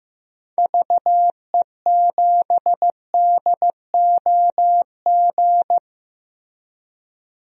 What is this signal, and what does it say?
Morse code, heard as a keyed tone, about 15 wpm, 700 Hz. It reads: VE7DOG
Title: DXCC-100 Call Signs - 6 Characters - 15wpm